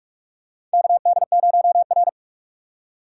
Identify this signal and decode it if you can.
Morse code, heard as a keyed tone, about 42 wpm, 690 Hz. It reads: KD0R
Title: USA Call Signs - 4 Characters - 2x Character Spacing - 45wpm